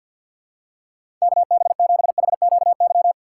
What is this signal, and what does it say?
Morse code, heard as a keyed tone, about 52 wpm, 700 Hz. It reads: KB7HQY
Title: USA Call Signs - 4x Character Spacing - 50wpm